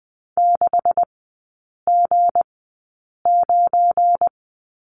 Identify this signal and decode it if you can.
Morse code, heard as a keyed tone, about 20 wpm, spaced as for 8 wpm, 700 Hz. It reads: 6G9
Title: 212 Lesson 34e - Number 8 - Three Characters 20wpm